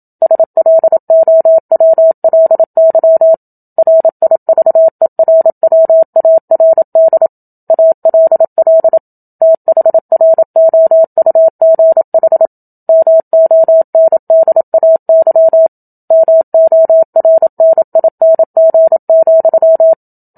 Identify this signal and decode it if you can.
Morse code, heard as a keyed tone, about 27 wpm, 650 Hz. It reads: SLOWLY RIVERWARD ALL THROUGH MONDAY MORNING,
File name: Book Two / 01-Chapter 1 - 27 WPM